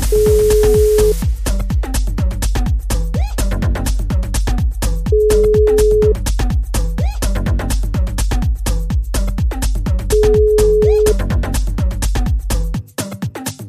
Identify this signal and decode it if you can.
Ringback tone (Europe, 425 Hz).